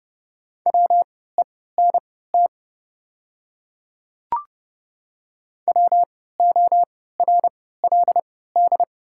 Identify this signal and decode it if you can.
Morse code, heard as a keyed tone, about 30 wpm, spaced as for 17 wpm, 700 Hz. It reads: WENT WORLD
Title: 3x Character Spacing - Top 200 Words - 30wpm